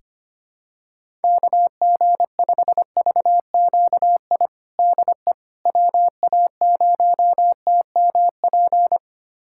Morse code, heard as a keyed tone, about 25 wpm, 700 Hz. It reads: KG5VQI DE WA0TMP
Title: USA Callsign DE Callsign 25wpm